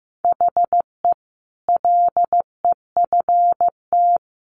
Morse code, heard as a keyed tone, about 15 wpm, 700 Hz. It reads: HE LEFT
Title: Sentences from Top 1100 Words 15wpm